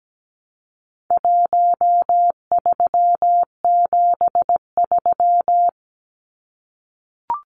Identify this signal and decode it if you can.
Morse code, heard as a keyed tone, about 17 wpm, 700 Hz. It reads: 1373
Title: Numbers - Contest Style - 4-Digit 17wpm